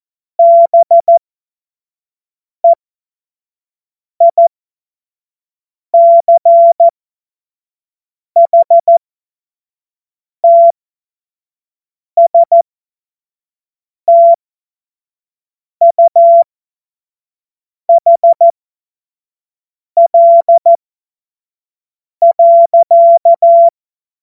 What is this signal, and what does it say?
Morse code, heard as a keyed tone, about 14 wpm, spaced as for 5 wpm, 680 Hz. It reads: BEICHTSTUHL.
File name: Morsekurs / Witze / Witz 01. Juni 2021, 25 F70